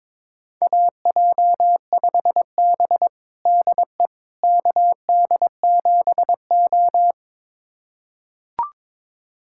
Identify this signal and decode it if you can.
Morse code, heard as a keyed tone, about 22 wpm, 700 Hz. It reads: AJ5B DE KD7O
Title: USA Callsign DE Callsign - 4 Characters - 22wpm